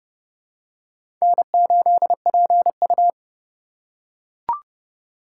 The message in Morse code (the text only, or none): N8PU